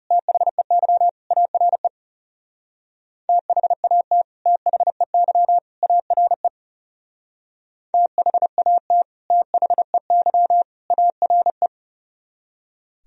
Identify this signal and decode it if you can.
Morse code, heard as a keyed tone, about 37 wpm, 700 Hz. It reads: THEY ARE THAT THEY ARE THAT THEY ARE
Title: Speed-Racing - Sets of 3 Words 30wpm